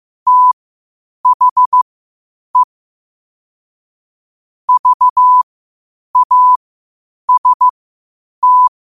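Morse code, heard as a keyed tone, about 15 wpm, spaced as for 8 wpm, 1000 Hz. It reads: THE VAST
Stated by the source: Morse practice at 8 WPM from Morse Resource.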